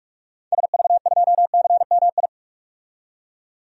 Morse code, heard as a keyed tone, about 46 wpm, 700 Hz. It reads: SV2CMI
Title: DXCC-100 Call Signs - 6 Characters - 45wpm